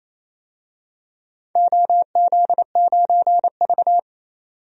Morse code, heard as a keyed tone, about 29 wpm, 700 Hz. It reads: OZ9V
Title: DXCC-100 Call Signs - 4 Characters - 28wpm